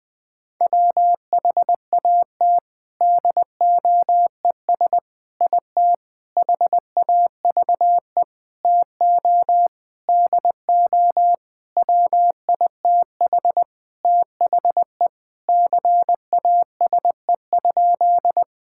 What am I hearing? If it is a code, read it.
Morse code, heard as a keyed tone, about 21 wpm, 700 Hz. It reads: WHAT DOES IT HAVE TO DO WITH THE CASE?
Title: Sentences with 4-Word Phrases - 20wpm